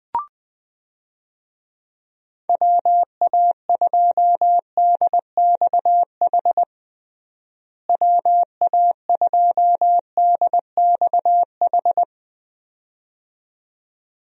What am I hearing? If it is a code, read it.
Morse code, heard as a keyed tone, about 21 wpm, 700 Hz. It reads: WA2DXH WA2DXH
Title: USA Call Signs - 6 Characters - Repeated 2x - 20wpm